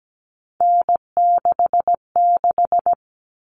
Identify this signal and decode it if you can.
Morse code, heard as a keyed tone, about 17 wpm, 700 Hz. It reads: N66
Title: All Cut Numbers - 3-Digit 17wpm